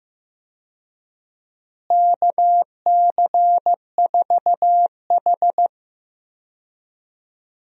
Morse code, heard as a keyed tone, about 15 wpm, 700 Hz. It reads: KC4H
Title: USA Call Signs - 4 Characters - Repeated 2x - 15wpm